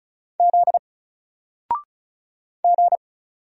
Morse code, heard as a keyed tone, about 36 wpm, 700 Hz. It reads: Z G
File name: Single-Letter-Number-35wpm